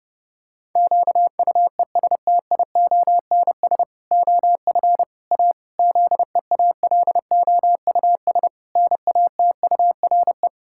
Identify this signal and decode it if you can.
Morse code, heard as a keyed tone, about 30 wpm, 700 Hz. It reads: QUESTIONS OF A ZEALOUS NATURE
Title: Pangram Challenge v2 - 30wpm